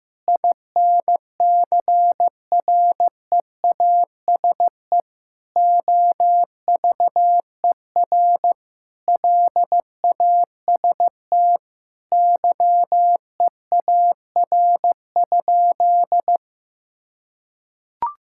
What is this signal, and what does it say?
Morse code, heard as a keyed tone, about 15 wpm, 700 Hz. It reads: INCREASE OVER LAST YEAR?
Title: Sentences from Top 1000 Words - Review Part 5 - 15wpm